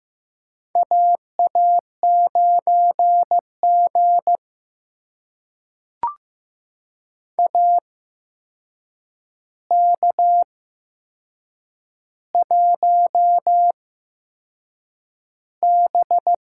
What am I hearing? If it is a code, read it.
Morse code, heard as a keyed tone, about 15 wpm, spaced as for 6 wpm, 700 Hz. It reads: AA9G AK1B
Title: USA Call Signs - 4 Characters - 8x Character Spacing - 15wpm